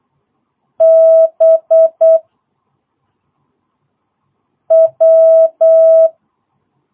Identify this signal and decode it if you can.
Morse code, heard as a keyed tone, about 8 wpm, 660 Hz. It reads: B W
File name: jh8cbh / 2022cw